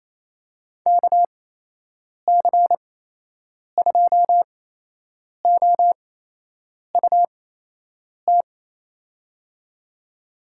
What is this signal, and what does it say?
Morse code, heard as a keyed tone, about 28 wpm, spaced as for 8 wpm, 700 Hz. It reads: KC2OUT
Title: USA Call Signs - 8x Character Spacing - 28wpm